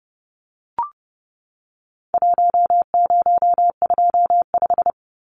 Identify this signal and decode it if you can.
Morse code, heard as a keyed tone, about 30 wpm, 700 Hz. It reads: 1025
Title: Numbers - Contest Style - 4-Digit 30wpm